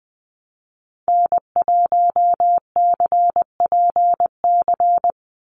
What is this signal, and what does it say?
Morse code, heard as a keyed tone, about 20 wpm, 700 Hz. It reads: N1CPC
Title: USA Call Signs 20wpm